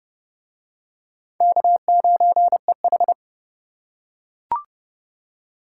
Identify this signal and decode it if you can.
Morse code, heard as a keyed tone, about 27 wpm, 700 Hz. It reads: K9EH